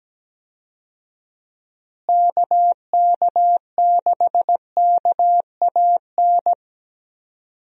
Morse code, heard as a keyed tone, about 17 wpm, 700 Hz. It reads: KK6KAN
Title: USA Call Signs - 6 Characters - Repeated 2x - 17wpm